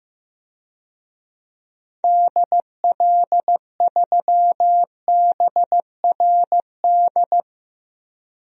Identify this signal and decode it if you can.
Morse code, heard as a keyed tone, about 15 wpm, 700 Hz. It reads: DL3BRD
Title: DXCC-100 Call Signs - 6 Characters - 15wpm